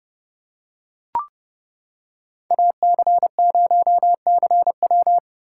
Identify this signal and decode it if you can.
Morse code, heard as a keyed tone, about 28 wpm, 700 Hz. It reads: AC0CW